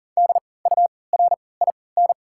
Morse code, heard as a keyed tone, about 40 wpm, spaced as for 23 wpm, 700 Hz. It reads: DURIN
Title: 3x Character Spacing - Top 500 Words - 40wpm